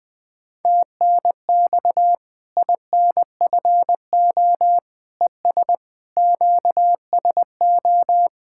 Morse code, heard as a keyed tone, about 20 wpm, 700 Hz. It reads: TNX INFO ES QSO
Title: Practice QSO - Exchange 4 - Ending - 20wpm